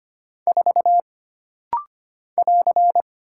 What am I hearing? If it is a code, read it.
Morse code, heard as a keyed tone, about 25 wpm, 700 Hz. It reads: 4 <AR>
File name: Single Character (Letter:Number:Punctuation:Prosign) - Rapid-Fire - 25wpm